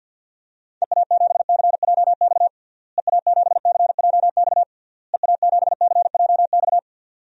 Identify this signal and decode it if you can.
Morse code, heard as a keyed tone, about 48 wpm, 700 Hz. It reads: EA7CJX EA7CJX EA7CJX
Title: DXCC-100 Call Signs - 6 Characters - Repeated 3x - 50wpm